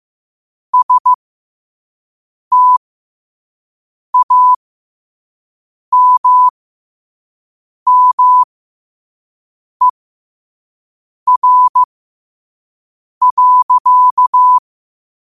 Morse code, heard as a keyed tone, about 15 wpm, spaced as for 5 wpm, 1000 Hz. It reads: STAMMER.
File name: qotdfiles / 5wpm